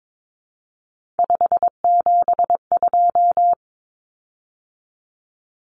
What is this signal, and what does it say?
Morse code, heard as a keyed tone, about 22 wpm, 700 Hz. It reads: 572
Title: Common Cut Numbers - 3-Digit 22wpm